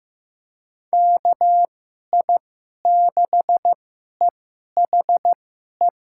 Morse code, heard as a keyed tone, about 15 wpm, spaced as for 11 wpm, 700 Hz. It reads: KI6EHE
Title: USA Call Signs - 6 Characters - 2x Character Spacing - 15wpm